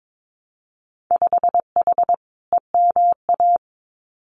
Morse code, heard as a keyed tone, about 22 wpm, 700 Hz. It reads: 5H EMA
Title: Winter Field Day Exchanges Encore 22wpm